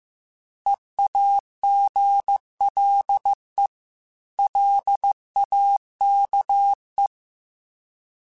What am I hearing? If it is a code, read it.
Morse code, heard as a keyed tone, about 15 wpm, 800 Hz. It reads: EAGLE LAKE